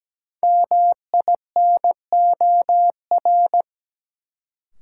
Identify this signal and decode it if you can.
Morse code, heard as a keyed tone, about 17 wpm, 700 Hz. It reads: MINOR
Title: Sentences from Top 1500 Words - New Word Drill Alt - 17wpm